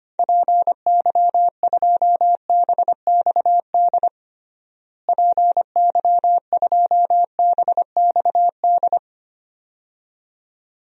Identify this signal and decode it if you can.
Morse code, heard as a keyed tone, about 25 wpm, 700 Hz. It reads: PY2BXD PY2BXD
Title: DXCC-100 Call Signs - 6 Characters - Repeated 2x - 25wpm